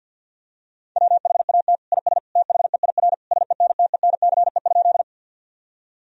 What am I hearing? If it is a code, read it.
Morse code, heard as a keyed tone, about 50 wpm, 700 Hz. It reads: WHAT IS THEIR SENTENCE?
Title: Sentences from Top 1000 Words - Review Part 1 - 50wpm